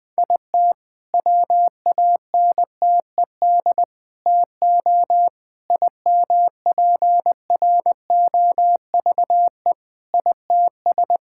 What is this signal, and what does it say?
Morse code, heard as a keyed tone, about 20 wpm, 700 Hz. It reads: IT WANTED TO IMPROVE ITS
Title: All-Age Appropriate Jokes - 20wpm